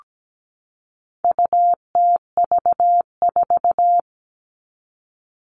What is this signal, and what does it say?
Morse code, heard as a keyed tone, about 17 wpm, 700 Hz. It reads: UTV4
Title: All Cut Numbers - 4-Digit 17wpm